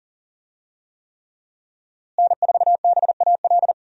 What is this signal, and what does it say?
Morse code, heard as a keyed tone, about 40 wpm, 700 Hz. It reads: N4BAL
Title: USA Call Signs - 5 Characters - Repeated 3x - 40wpm